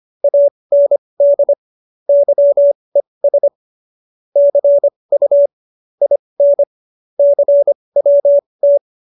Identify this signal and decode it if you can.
Morse code, heard as a keyed tone, about 25 wpm, spaced as for 20 wpm, 570 Hz. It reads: AND YES CU IN CWT
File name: QSO316 20